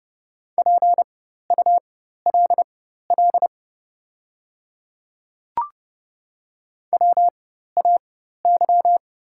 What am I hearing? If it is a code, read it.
Morse code, heard as a keyed tone, about 30 wpm, spaced as for 14 wpm, 690 Hz. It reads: PULL WAY
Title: 4x Character Spacing - Top 500 Words - 30wpm